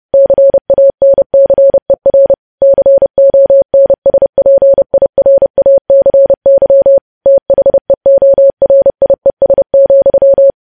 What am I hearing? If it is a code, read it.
Morse code, heard as a keyed tone, about 30 wpm, 550 Hz. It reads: CANCER CONSPIRACY THEORIES,
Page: news headlines in morse code at 30 wpm